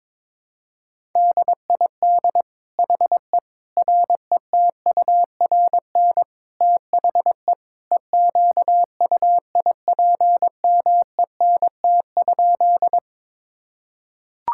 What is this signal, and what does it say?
Morse code, heard as a keyed tone, about 22 wpm, 700 Hz. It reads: DID HE RETURN THE EQUIPMENT?
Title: Sentences from Top 1000 Words - Review Part 5 - 22wpm